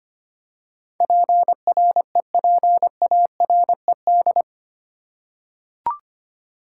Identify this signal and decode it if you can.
Morse code, heard as a keyed tone, about 25 wpm, 700 Hz. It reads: PREPARED
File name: 8 Letter Words - First 3 Letters Given - 25wpm